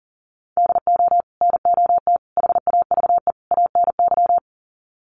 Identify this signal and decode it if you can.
Morse code, heard as a keyed tone, about 40 wpm, 700 Hz. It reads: DO NOT HAVE ANY
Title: Sets of 4 Words 40wpm